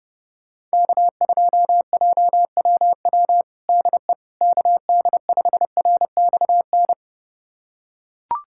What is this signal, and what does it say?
Morse code, heard as a keyed tone, about 32 wpm, 700 Hz. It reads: K2JWW DE KD5RXN
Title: USA Callsign DE Callsign 30wpm